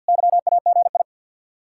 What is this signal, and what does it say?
Morse code, heard as a keyed tone, about 65 wpm, spaced as for 57 wpm, 700 Hz. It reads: YAGI
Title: Speed-Racing Common QSO Phrases - 50wpm